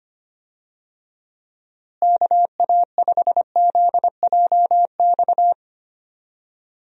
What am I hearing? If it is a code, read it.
Morse code, heard as a keyed tone, about 25 wpm, 700 Hz. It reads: KA5ZJX
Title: USA Call Signs - 6 Characters - Repeated 3x - 25wpm